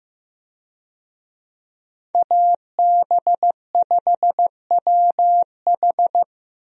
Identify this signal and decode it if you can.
Morse code, heard as a keyed tone, about 15 wpm, 700 Hz. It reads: AB5WH
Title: USA Call Signs - 5 Characters - Repeated 2x - 15wpm